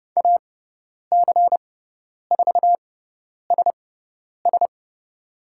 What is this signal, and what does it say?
Morse code, heard as a keyed tone, about 30 wpm, spaced as for 10 wpm, 700 Hz. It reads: AC4SS
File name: Lesson 18g - Letter m - Call Signs 30wpm